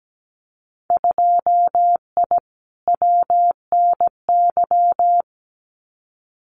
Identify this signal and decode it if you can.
Morse code, heard as a keyed tone, about 17 wpm, 700 Hz. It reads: 2I WNY
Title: Winter Field Day Exchanges 17wpm